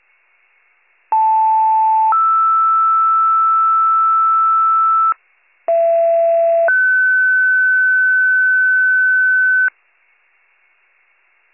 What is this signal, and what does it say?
Two-tone page, heard as a one-second tone, then a three-second tone, about 880 then 1320 Hz, about 670 then 1500 Hz. It is heard from 1.1 to 9.7 s.